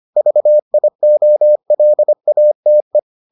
Morse code, heard as a keyed tone, about 25 wpm, 600 Hz. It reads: VIOLATE